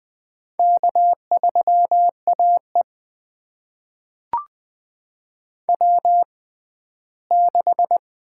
Morse code, heard as a keyed tone, about 20 wpm, spaced as for 14 wpm, 700 Hz. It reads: K3AE W6
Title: USA Call Signs - 4 Characters - 6x Character Spacing - 20wpm